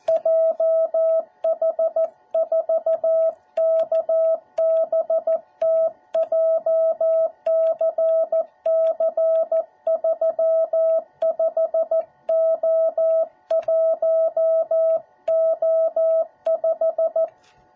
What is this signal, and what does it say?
Morse code, heard as a keyed tone, about 14 wpm, 650 Hz. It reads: JH4KBT JCC35O1O5